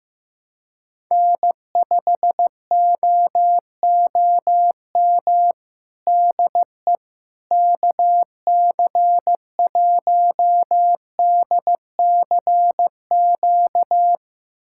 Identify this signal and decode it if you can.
Morse code, heard as a keyed tone, about 16 wpm, 700 Hz. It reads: N5OOM DE KC1DCQ
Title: USA Callsign DE Callsign 15wpm